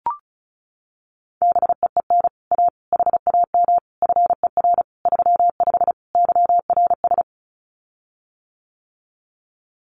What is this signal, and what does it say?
Morse code, heard as a keyed tone, about 34 wpm, 700 Hz. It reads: BEEN A HAM FER 35 YRS
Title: Common QSO Phrases Encore 35wpm